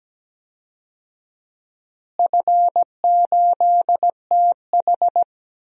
Morse code, heard as a keyed tone, about 17 wpm, 700 Hz. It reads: F8TH
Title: DXCC-100 Call Signs - 4 Characters - Repeated 2x - 17wpm